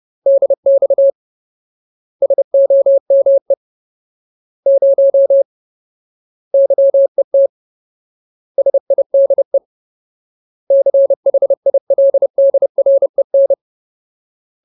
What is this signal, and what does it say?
Morse code, heard as a keyed tone, about 30 wpm, 550 Hz. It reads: DX SOME 0 YET SIDE CHILDREN
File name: Ultimate Word List / 30WPM Ultimate